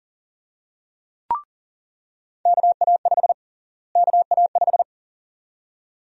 Morse code, heard as a keyed tone, about 40 wpm, 700 Hz. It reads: KA5 KA5
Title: USA Call Sign Prefixes - Repeated 2x - 40wpm